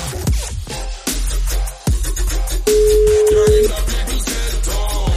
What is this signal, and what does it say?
Ringback tone (Europe, 425 Hz).